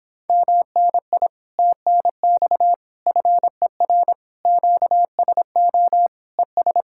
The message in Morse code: MNI TNX FER QSO ES

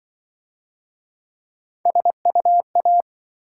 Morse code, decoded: SUA

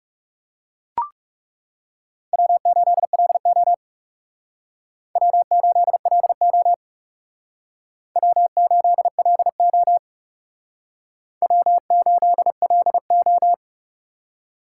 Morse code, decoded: W8LO W8LO W8LO W8LO

